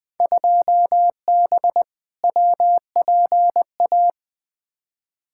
Morse code, heard as a keyed tone, about 20 wpm, 700 Hz. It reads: 2B WPA
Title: Field Day Exchanges Encore - 20wpm